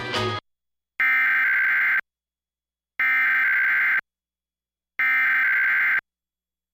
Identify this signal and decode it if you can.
Emergency Alert System (SAME) tones (header bursts), heard as harsh buzzing bursts. It is heard from 1.0 to 6.0 s.